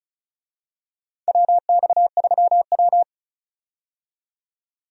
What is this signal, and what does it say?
Morse code, heard as a keyed tone, about 34 wpm, 700 Hz. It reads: WX3W
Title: USA Call Signs - 4 Characters - 35wpm